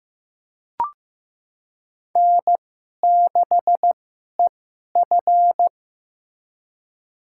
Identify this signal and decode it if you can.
Morse code, heard as a keyed tone, about 15 wpm, spaced as for 11 wpm, 700 Hz. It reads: N6EF
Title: USA Call Signs - 4 Characters - 2x Character Spacing - 15wpm